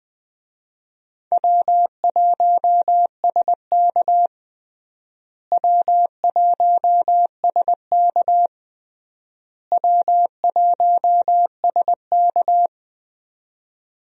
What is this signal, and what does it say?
Morse code, heard as a keyed tone, about 20 wpm, 700 Hz. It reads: W1SK W1SK W1SK
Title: USA Call Signs - 4 Characters - Repeated 3x - 20wpm